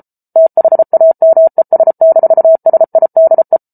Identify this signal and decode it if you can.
Morse code, heard as a keyed tone, about 34 wpm, 650 Hz. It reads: THAMES-SIDE